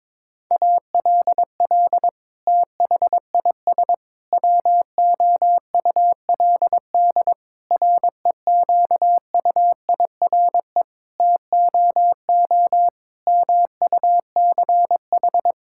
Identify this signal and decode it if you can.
Morse code, heard as a keyed tone, about 22 wpm, 700 Hz. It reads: ALL THIS WOULD REQUIRE TOO MUCH
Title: Sentences from Top 1000 Words - Review Part 6 - 22wpm